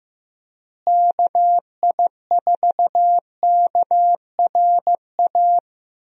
Morse code, heard as a keyed tone, about 15 wpm, 700 Hz. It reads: KI4KRA
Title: USA Call Signs - 6 Characters - Repeated 3x - 15wpm